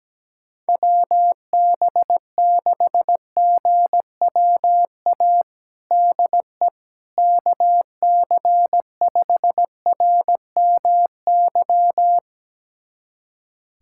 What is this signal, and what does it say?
Morse code, heard as a keyed tone, about 17 wpm, 700 Hz. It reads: WB6GWA DE KC5RMY
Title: USA Callsign DE Callsign 17wpm